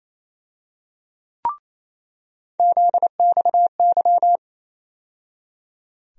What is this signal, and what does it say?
Morse code, heard as a keyed tone, about 28 wpm, 700 Hz. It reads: ZXY
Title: USA Call Sign Postfixes - 3 Letters 28wpm